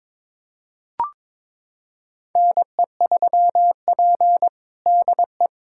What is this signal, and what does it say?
Morse code, heard as a keyed tone, about 22 wpm, 700 Hz. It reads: NE3P DE
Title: USA Callsign DE Callsign - 4 Characters - 22wpm